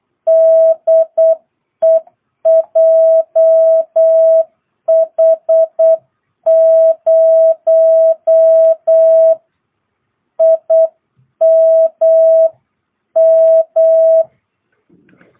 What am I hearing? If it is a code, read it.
Morse code, heard as a keyed tone, about 8 wpm, spaced as for 7 wpm, 660 Hz. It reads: DE JH0IMM
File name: jh8cbh / 2022cw